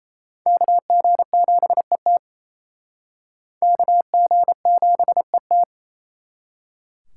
Morse code, heard as a keyed tone, about 34 wpm, 700 Hz. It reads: KG7ET KG7ET
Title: Speed Racing - USA 2x2 Call Signs 28wpm